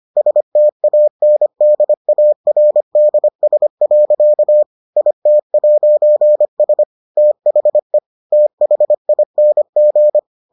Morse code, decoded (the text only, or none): STANDARDS. IT'S THE THING